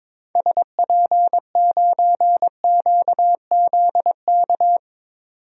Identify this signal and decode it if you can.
Morse code, heard as a keyed tone, about 22 wpm, 700 Hz. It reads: SP9QZK
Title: DXCC-100 Call Signs - 6 Characters - 22wpm